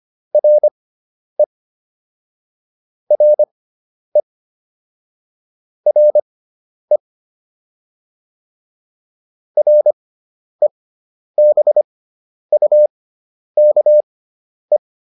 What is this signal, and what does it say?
Morse code, heard as a keyed tone, about 25 wpm, spaced as for 10 wpm, 600 Hz. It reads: RE RE RE REBUKE